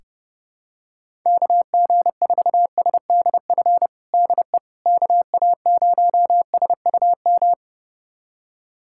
Morse code, heard as a keyed tone, about 32 wpm, 700 Hz. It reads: KG4SDF DE KA0SUM
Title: USA Callsign DE Callsign - 6 Characters - 30wpm